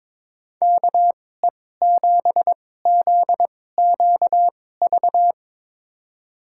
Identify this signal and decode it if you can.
Morse code, heard as a keyed tone, about 22 wpm, spaced as for 16 wpm, 700 Hz. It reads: KE7ZQV
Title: USA Call Signs - 2x Character Spacing - 22wpm